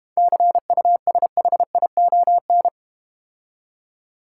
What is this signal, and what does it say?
Morse code, heard as a keyed tone, about 31 wpm, 700 Hz. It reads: CUSHION
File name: Easy Words - Speed-Racing with Hints - 25wpm